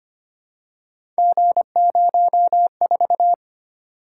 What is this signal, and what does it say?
Morse code, heard as a keyed tone, about 25 wpm, 700 Hz. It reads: G04
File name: Three Letter-Number Combinations - 25wpm